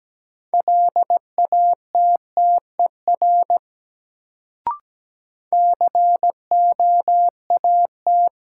Morse code, heard as a keyed tone, about 17 wpm, 700 Hz. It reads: LATTER COAT
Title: Sentences from Top 1600 Words - New Word Drill Alt - 17wpm